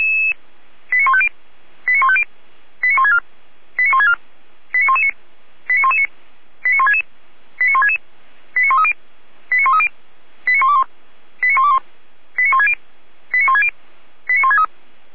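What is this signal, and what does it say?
Five-tone selective call (ZVEI), heard as quick runs of five tones, tones of about 70 ms. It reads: 89150 89150 89164 89164 89119 89119 89166 89166 89130 89130 89121 89121 89160 89160 89163
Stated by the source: ZVEI five-tone selective call received by SDR.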